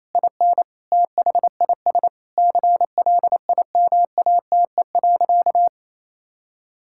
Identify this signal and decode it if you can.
Morse code, heard as a keyed tone, about 27 wpm, 700 Hz. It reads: IN THIS CLIMATE.
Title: Sentences from Top 1600 Words 28wpm